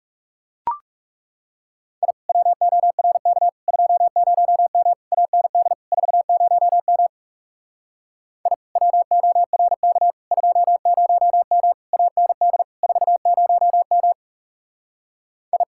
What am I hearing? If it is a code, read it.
Morse code, heard as a keyed tone, about 37 wpm, 700 Hz. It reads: I WORK 20M AND 40M I WORK 20M AND 40M I
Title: Speed-Racing Common QSO Phrases Encore - 30wpm